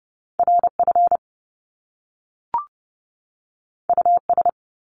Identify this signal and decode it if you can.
Morse code, heard as a keyed tone, about 30 wpm, 700 Hz. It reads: RF US
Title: Top Two Letter Combinations in Words 30wpm